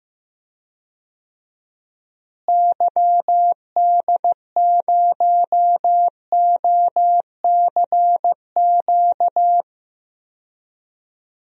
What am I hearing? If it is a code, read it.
Morse code, heard as a keyed tone, about 16 wpm, 700 Hz. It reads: YD0OCQ
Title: DXCC-100 Call Signs - 6 Characters - Repeated 3x - 15wpm